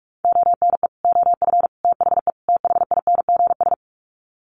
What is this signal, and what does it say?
Morse code, heard as a keyed tone, about 44 wpm, 700 Hz. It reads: ONE OF THE THINGS
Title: Sets of 4 Words 45wpm